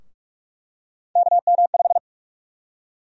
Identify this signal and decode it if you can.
Morse code, heard as a keyed tone, about 46 wpm, 700 Hz. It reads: KM5
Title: USA Call Sign Prefixes - Repeated 2x - 45wpm